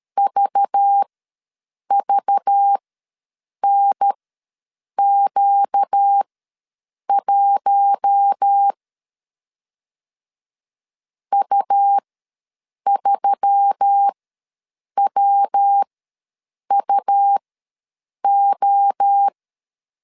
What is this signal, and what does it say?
Morse code, heard as a keyed tone, about 13 wpm, spaced as for 7 wpm, 800 Hz. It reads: VVNQ1 U3WUO
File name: LEZIONI / 5 LBVC / 5 LBV ws 12wpm 3015